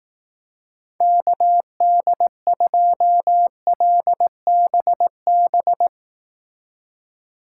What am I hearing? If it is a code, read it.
Morse code, heard as a keyed tone, about 18 wpm, 700 Hz. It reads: KD2LBB
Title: Speed Racing - USA 2x3 Call Signs 15wpm